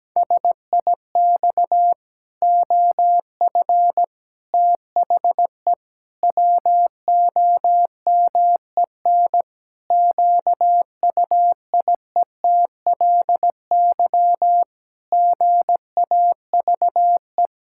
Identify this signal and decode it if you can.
Morse code, heard as a keyed tone, about 17 wpm, 700 Hz. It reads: SIX OF THE WOMEN QUIETLY GAVE
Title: Pangram Challenge v2 - 17wpm